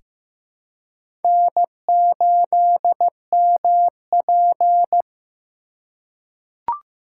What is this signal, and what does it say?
Morse code, heard as a keyed tone, about 16 wpm, 700 Hz. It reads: N8MP
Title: USA Call Signs - 4x Character Spacing - 15wpm